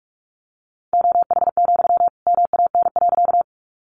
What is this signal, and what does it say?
Morse code, heard as a keyed tone, about 44 wpm, 700 Hz. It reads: OH, MAN.
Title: Sentences from Top 500 Words 45wpm